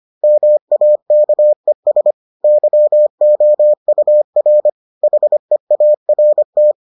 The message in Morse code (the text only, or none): MAKES YOUR HEART